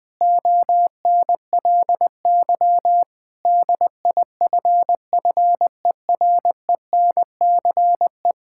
Morse code, heard as a keyed tone, about 21 wpm, 700 Hz. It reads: ONLY DIFFERENCE
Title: Sentences from Top 1000 Words - Review Part 5 - 20wpm